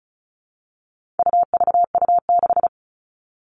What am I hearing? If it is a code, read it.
Morse code, heard as a keyed tone, about 34 wpm, 700 Hz. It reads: UVU6
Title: All Cut Numbers - 4-Digit 35wpm